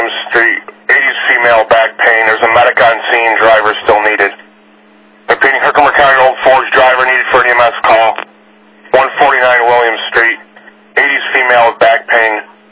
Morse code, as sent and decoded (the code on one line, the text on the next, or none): none